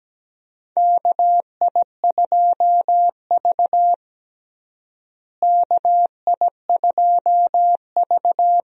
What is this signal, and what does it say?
Morse code, heard as a keyed tone, about 17 wpm, 700 Hz. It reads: KI2V KI2V
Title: USA Call Signs - 4 Characters - Repeated 3x - 17wpm